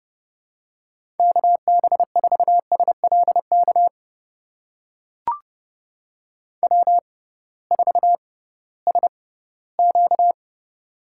Morse code, heard as a keyed tone, about 30 wpm, spaced as for 17 wpm, 700 Hz. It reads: KB4SLK W4SQ
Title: USA Call Signs - 6x Character Spacing - 30wpm